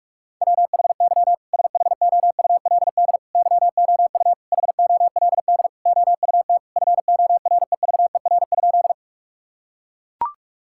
Morse code, heard as a keyed tone, about 41 wpm, 700 Hz. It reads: WHY SHOULD YOU HOLD OUT FOREVER?
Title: Sentences from Top 1600 Words 45wpm